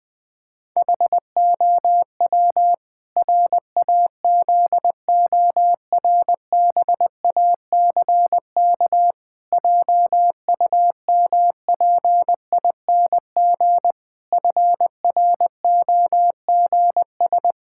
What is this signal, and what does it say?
Morse code, heard as a keyed tone, about 20 wpm, 700 Hz. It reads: HOW RAZORBACK JUMPING FROGS
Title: Pangram Challenge v2 - 20wpm